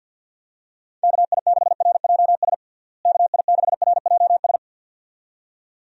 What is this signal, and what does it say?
Morse code, heard as a keyed tone, about 49 wpm, 700 Hz. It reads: KI6RJS KI6RJS
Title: USA Call Signs - 6 Characters - Repeated 2x - 50wpm